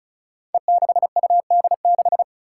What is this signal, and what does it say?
Morse code, heard as a keyed tone, about 34 wpm, 700 Hz. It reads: E6UD6
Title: Five Challenging Letter-Number Combinations 35wpm